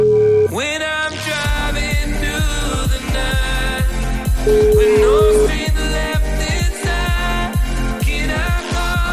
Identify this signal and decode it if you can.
Ringback tone (Europe, 425 Hz), with noise.